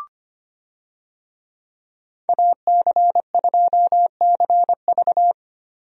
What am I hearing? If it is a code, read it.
Morse code, heard as a keyed tone, about 25 wpm, 700 Hz. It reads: AC2CV
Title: USA Call Signs - 5 Characters - Repeated 3x - 25wpm